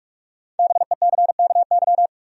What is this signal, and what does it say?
Morse code, heard as a keyed tone, about 47 wpm, 700 Hz. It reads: BECKY
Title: USA Common Names - Max 5 Letters - 45wpm